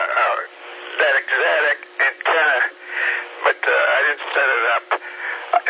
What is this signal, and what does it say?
Emergency Alert System (SAME) tones, heard as harsh buzzing bursts, with speech. It is heard from 0.9 to 4.6 s.